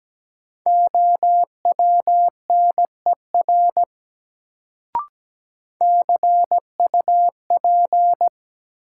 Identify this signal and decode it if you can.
Morse code, heard as a keyed tone, about 17 wpm, 700 Hz. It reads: OWNER CUP
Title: Sentences from Top 1600 Words - New Word Drill Alt - 17wpm